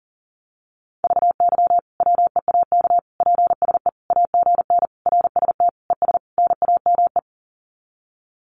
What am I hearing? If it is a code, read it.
Morse code, heard as a keyed tone, about 40 wpm, 700 Hz. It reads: VY WEAK PSE AGN RST ES NAME
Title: Common QSO Phrases 40wpm